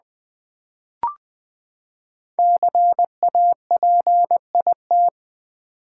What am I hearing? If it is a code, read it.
Morse code, heard as a keyed tone, about 20 wpm, 700 Hz. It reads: CAPIT
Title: Build A Word - 5 Letters At A Time - 10-Letter Words - 20wpm